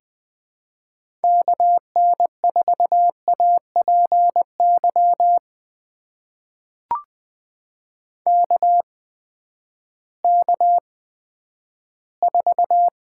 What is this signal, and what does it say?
Morse code, heard as a keyed tone, about 20 wpm, spaced as for 11 wpm, 700 Hz. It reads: KN4APY KK4
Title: USA Call Signs - 6 Characters - 8x Character Spacing - 20wpm